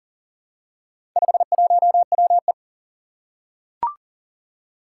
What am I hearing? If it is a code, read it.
Morse code, heard as a keyed tone, about 40 wpm, 700 Hz. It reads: 51WE
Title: Four Challenging Letter-Number Combinations 40wpm